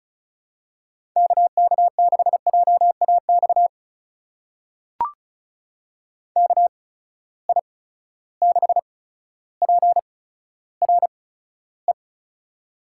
Morse code, heard as a keyed tone, about 34 wpm, spaced as for 15 wpm, 700 Hz. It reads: KK6JAX KI6PRE
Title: USA Call Signs - 6 Characters - 8x Character Spacing - 35wpm